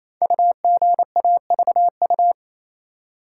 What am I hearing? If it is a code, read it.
Morse code, heard as a keyed tone, about 28 wpm, 700 Hz. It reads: UGAVU
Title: Code Groups - 5 Letter:Numbers - 28wpm